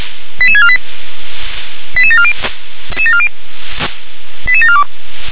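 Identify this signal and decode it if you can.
Five-tone selective call (ZVEI), heard as quick runs of five tones, tones of about 70 ms. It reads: 88648 88740 88742
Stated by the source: ZVEI five-tone call received by SDR.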